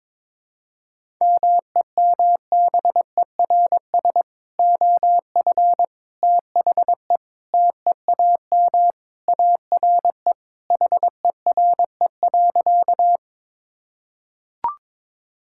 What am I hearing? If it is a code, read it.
Morse code, heard as a keyed tone, about 22 wpm, 700 Hz. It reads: MEMBERS OF THE TEAM ARE HERE.
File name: practice / 22wpm / Sentences with 3-Word Phrases - 22wpm